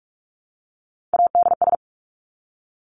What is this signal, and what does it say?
Morse code, heard as a keyed tone, about 46 wpm, 700 Hz. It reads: ADS
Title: Abbreviations 45wpm